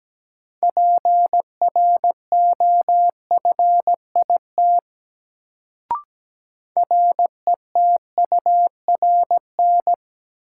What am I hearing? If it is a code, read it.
Morse code, heard as a keyed tone, about 17 wpm, 700 Hz. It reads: PROFIT RETURN
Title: Sentences from Top 800 Words - New Word Drill - 17wpm